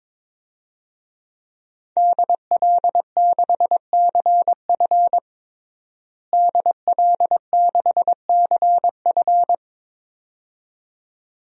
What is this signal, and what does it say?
Morse code, heard as a keyed tone, about 22 wpm, 700 Hz. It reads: DL6CF DL6CF
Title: DXCC-100 Call Signs - 5 Characters - Repeated 2x - 22wpm